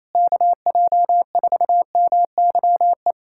Morse code, heard as a keyed tone, about 27 wpm, 700 Hz. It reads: KJ4MYE
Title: USA Callsign DE Callsign - 6 Characters - 28wpm